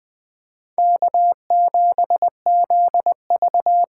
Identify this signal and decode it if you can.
Morse code, heard as a keyed tone, about 23 wpm, 700 Hz. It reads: K7ZV
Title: Speed Racing - USA 1x2 Call Signs 20wpm